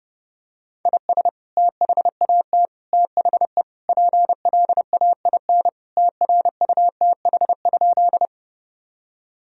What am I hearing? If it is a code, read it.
Morse code, heard as a keyed tone, about 30 wpm, 700 Hz. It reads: IS THAT THE PLAIN TRUTH?
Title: Sentences from Top 1000 Words - Review Part 2 - 30wpm